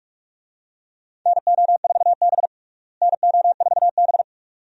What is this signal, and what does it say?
Morse code, heard as a keyed tone, about 46 wpm, 700 Hz. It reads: NO4B NO4B
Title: USA Call Signs - 4 Characters - Repeated 2x - 45wpm